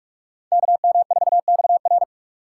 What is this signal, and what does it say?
Morse code, heard as a keyed tone, about 58 wpm, 700 Hz. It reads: KM4XR